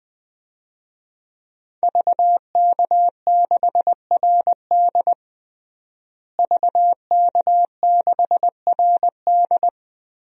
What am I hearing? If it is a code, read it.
Morse code, heard as a keyed tone, about 21 wpm, 700 Hz. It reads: VK6RD VK6RD
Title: DXCC-100 Call Signs - 5 Characters - Repeated 3x - 20wpm